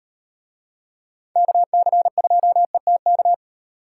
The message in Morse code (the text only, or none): KC2ETK